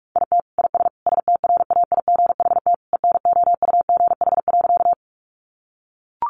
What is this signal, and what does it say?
Morse code, heard as a keyed tone, about 44 wpm, 700 Hz. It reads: IT IS STRAIGHT ENOUGH.
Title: Sentences from Top 1000 Words 45wpm